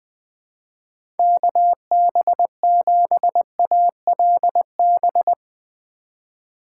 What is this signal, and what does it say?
Morse code, heard as a keyed tone, about 20 wpm, 700 Hz. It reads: KB7ALB
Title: USA Call Signs - 6 Characters - 3x Character Spacing - 20wpm